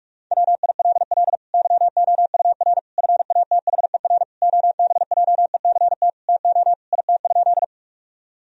Morse code, heard as a keyed tone, about 46 wpm, 700 Hz. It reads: WILL YOUR FATHER OBJECT TO IT?
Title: Sentences from Top 1000 Words - Review Part 3 - 45wpm